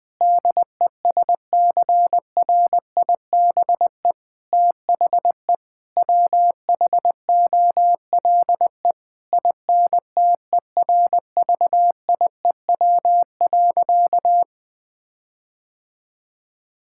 Morse code, heard as a keyed tone, about 21 wpm, 700 Hz. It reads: DESCRIBE THE WHOLE INTERVIEW.
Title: Sentences from Top 1600 Words 20wpm